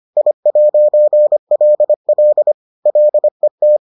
Morse code, heard as a keyed tone, about 25 wpm, 600 Hz. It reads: I'LL LET